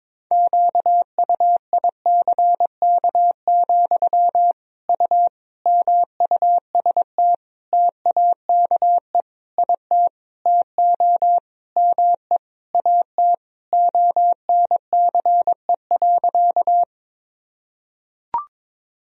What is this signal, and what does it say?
Morse code, heard as a keyed tone, about 22 wpm, 700 Hz. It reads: QUICK, U MUST TAKE IT TO ME AT ONCE.
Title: Sentences from Top 500 Words and Abbreviations 22wpm